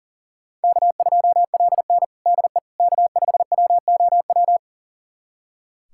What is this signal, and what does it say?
Morse code, heard as a keyed tone, about 40 wpm, 700 Hz. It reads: K2LN DE K5WOW
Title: USA Callsign DE Callsign 40wpm